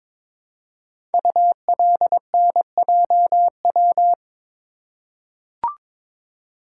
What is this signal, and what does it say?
Morse code, heard as a keyed tone, about 22 wpm, 700 Hz. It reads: ULNJW
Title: Code Groups - 5 Letters - 22wpm